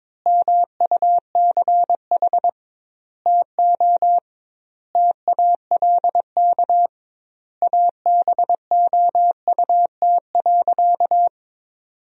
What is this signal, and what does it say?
Morse code, heard as a keyed tone, about 22 wpm, 700 Hz. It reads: MUCH TO TALK ABOUT.
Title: 2x Word-Spacing - Sentences from Top 400 Words - 22wpm